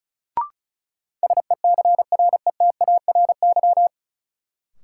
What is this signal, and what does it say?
Morse code, heard as a keyed tone, about 34 wpm, 700 Hz. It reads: SECRETARY
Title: Sentences from Top 1300 Words - New Word Drill Alt - 35wpm